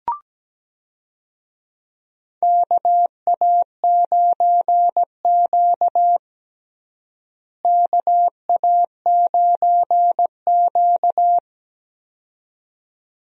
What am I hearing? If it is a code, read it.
Morse code, heard as a keyed tone, about 17 wpm, 700 Hz. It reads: KA9Q KA9Q
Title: USA Call Signs - 4 Characters - Repeated 2x - 17wpm